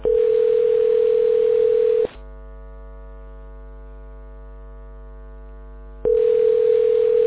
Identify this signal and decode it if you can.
Ringback tone (North America), 440 + 480 Hz.